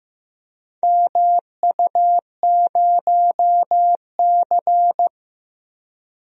Morse code, heard as a keyed tone, about 15 wpm, 700 Hz. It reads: MU0C
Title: DXCC-100 Call Signs - 4 Characters - 15wpm